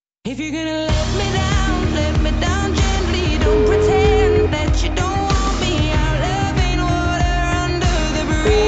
Ringback tone (Europe, 425 Hz), with noise.